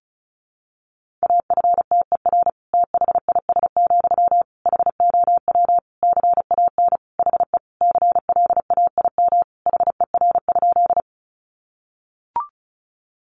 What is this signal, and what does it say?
Morse code, heard as a keyed tone, about 34 wpm, 700 Hz. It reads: AFTER THIS, HOW CAN HE CLAIM HER?
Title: Sentences from Top 1500 Words 35wpm